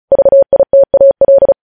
Morse code, heard as a keyed tone, about 36 wpm, 550 Hz. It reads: VITAL
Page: news headlines in morse code at 35 wpm